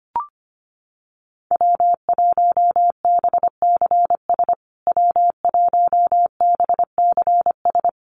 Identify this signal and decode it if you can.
Morse code, heard as a keyed tone, about 25 wpm, 700 Hz. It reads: W1BCS W1BCS
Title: USA Call Signs Repeated 3x 25wpm